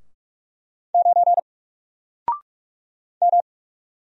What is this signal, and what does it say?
Morse code, heard as a keyed tone, about 46 wpm, 700 Hz. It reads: 9 M
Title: Single Letter-Number Alt 45wpm